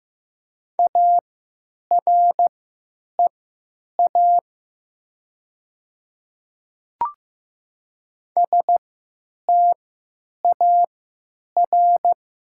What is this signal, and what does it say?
Morse code, heard as a keyed tone, about 15 wpm, spaced as for 9 wpm, 700 Hz. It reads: AREA STAR
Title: 3x Character Spacing - Top 500 Words - 15wpm